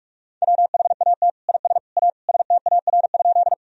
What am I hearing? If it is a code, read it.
Morse code, heard as a keyed tone, about 46 wpm, 700 Hz. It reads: WHAT IS A STAR?
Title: Sentences from Top 1000 Words - Review Part 2 - 45wpm